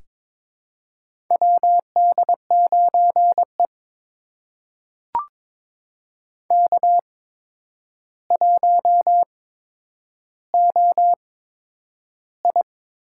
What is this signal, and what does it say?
Morse code, heard as a keyed tone, about 22 wpm, spaced as for 9 wpm, 700 Hz. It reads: WD9E K1OI
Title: USA Call Signs - 8x Character Spacing - 22wpm